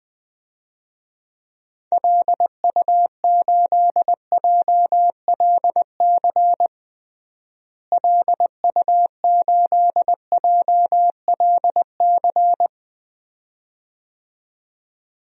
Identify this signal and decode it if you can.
Morse code, heard as a keyed tone, about 21 wpm, 700 Hz. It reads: LU8JLC LU8JLC
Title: DXCC-100 Call Signs - 6 Characters - Repeated 2x - 20wpm